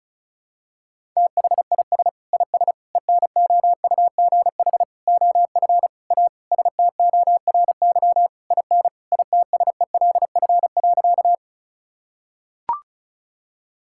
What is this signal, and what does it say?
Morse code, heard as a keyed tone, about 36 wpm, 700 Hz. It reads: THIS IS ENOUGH OF A STORY IN ITSELF.
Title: Sentences from Top 1000 Words - Review Part 5 - 35wpm